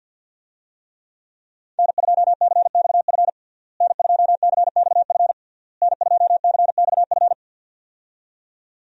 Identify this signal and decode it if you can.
Morse code, heard as a keyed tone, about 52 wpm, 700 Hz. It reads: N2CXF N2CXF N2CXF
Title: USA Call Signs - 5 Characters - Repeated 3x - 50wpm